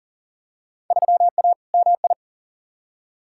Morse code, heard as a keyed tone, about 37 wpm, 700 Hz. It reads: 3A MI
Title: Field Day Exchanges - 40wpm